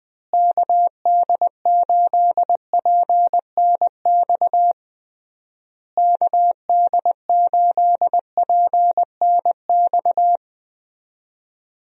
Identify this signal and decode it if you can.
Morse code, heard as a keyed tone, about 20 wpm, 700 Hz. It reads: KD8PNX KD8PNX
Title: USA Call Signs - 6 Characters - Repeated 3x - 20wpm